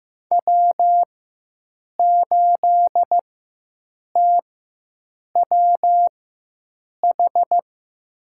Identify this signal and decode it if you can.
Morse code, heard as a keyed tone, about 15 wpm, spaced as for 7 wpm, 700 Hz. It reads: W8TWH
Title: USA Call Signs - 5 Characters - 4x Character Spacing - 15wpm